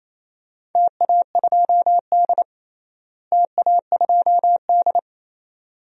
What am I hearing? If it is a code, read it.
Morse code, heard as a keyed tone, about 28 wpm, 700 Hz. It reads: TA2D TA2D
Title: DXCC-100 Call Signs - 4 Characters - Repeated 2x - 28wpm